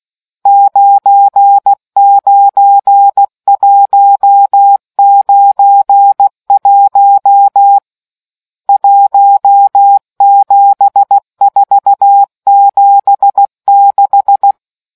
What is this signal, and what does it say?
Morse code, heard as a keyed tone, about 16 wpm, 800 Hz. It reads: 99191 17476